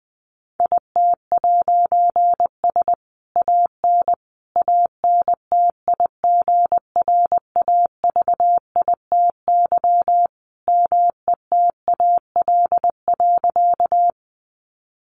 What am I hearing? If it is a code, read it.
Morse code, heard as a keyed tone, about 20 wpm, 700 Hz. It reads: IT'S AN ANTIGRAVITY METAL.
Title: A Sound Of Thunder 20wpm